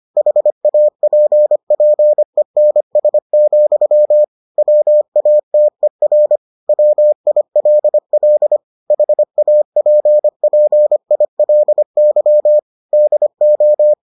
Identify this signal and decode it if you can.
Morse code, heard as a keyed tone, about 25 wpm, 600 Hz. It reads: HAPPENS, WATER WILL HAPPILY DO